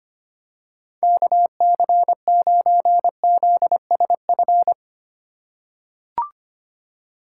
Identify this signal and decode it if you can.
Morse code, heard as a keyed tone, about 25 wpm, 700 Hz. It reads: KC9ZSF
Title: USA Call Signs - 6 Characters - 4x Character Spacing - 25wpm